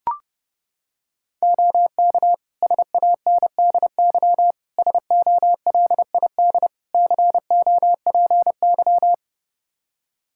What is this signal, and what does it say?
Morse code, heard as a keyed tone, about 30 wpm, 700 Hz. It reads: OK SANDY SOLID COPY
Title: Practice QSO - Exchange 3 - More Info - 30wpm